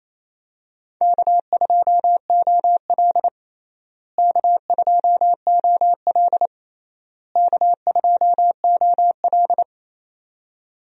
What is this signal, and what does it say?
Morse code, heard as a keyed tone, about 27 wpm, 700 Hz. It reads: K2OL K2OL K2OL
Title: USA Call Signs - 4 Characters - Repeated 3x - 28wpm